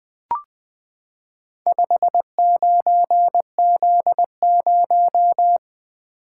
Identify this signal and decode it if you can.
Morse code, heard as a keyed tone, about 20 wpm, 700 Hz. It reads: 59Z0
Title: Four Letter-Number Combinations - 20wpm